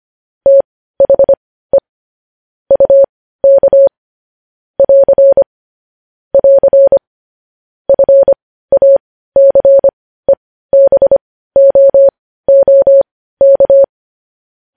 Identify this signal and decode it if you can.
Morse code, heard as a keyed tone, about 25 wpm, spaced as for 15 wpm, 550 Hz. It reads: THE UK <AR> <AR> FACEBOOK
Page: news headlines in morse code at 15 wpm